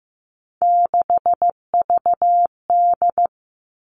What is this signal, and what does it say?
Morse code, heard as a keyed tone, about 15 wpm, 700 Hz. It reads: 6VD
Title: All Cut Numbers - 3-Digit 15wpm